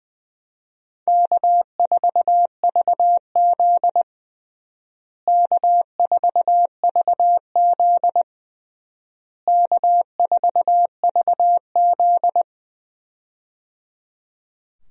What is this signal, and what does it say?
Morse code, heard as a keyed tone, about 21 wpm, 700 Hz. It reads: K4VZ K4VZ K4VZ
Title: USA Call Signs - 4 Characters - Repeated 3x - 20wpm